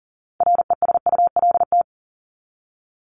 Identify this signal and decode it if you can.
Morse code, heard as a keyed tone, about 40 wpm, 700 Hz. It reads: RESULT
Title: ING Suffix - Top 100 Words with Base Words 40wpm